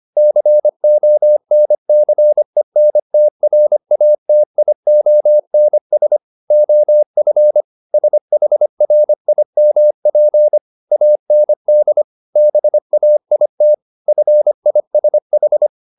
Morse code, heard as a keyed tone, about 25 wpm, 600 Hz. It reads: CONCENTRATIONS OF SHRIMP AND BAIT FISH
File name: the old man and the sea part 1 25wpm